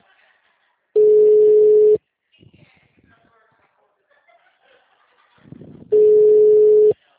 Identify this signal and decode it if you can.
Ringback tone (Europe, 425 Hz), with speech.
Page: Atibaia - SP